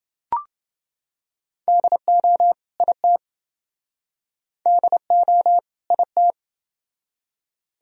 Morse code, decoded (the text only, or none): DO IT DO IT